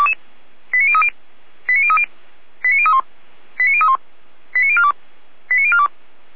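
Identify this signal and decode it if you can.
Five-tone selective call (ZVEI), heard as quick runs of five tones, tones of about 80 ms. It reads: 89030 89030 89031 89031 89042 89042
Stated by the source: ZVEI five-tone selective call received with an SDR.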